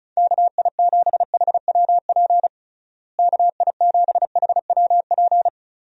Morse code, heard as a keyed tone, about 34 wpm, 700 Hz. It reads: KI7HWP KI7HWP
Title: USA Call Signs - 6 Characters - Repeated 3x - 35wpm